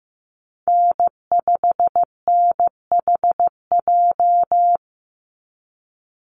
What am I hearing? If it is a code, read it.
Morse code, heard as a keyed tone, about 15 wpm, 700 Hz. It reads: N5NHJ
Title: Call Signs Repeated 2x - 15wpm